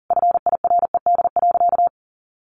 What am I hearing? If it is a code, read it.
Morse code, heard as a keyed tone, about 40 wpm, 700 Hz. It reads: FIRED.
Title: A Sound Of Thunder 40wpm